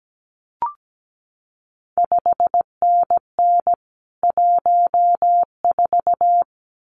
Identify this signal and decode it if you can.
Morse code, heard as a keyed tone, about 17 wpm, 700 Hz. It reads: 5NN 14
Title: CQ WW DX Contest Exchange - 17wpm